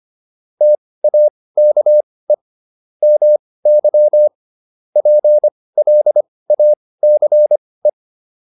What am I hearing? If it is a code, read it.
Morse code, heard as a keyed tone, about 25 wpm, spaced as for 18 wpm, 600 Hz. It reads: TAKE MY PLACE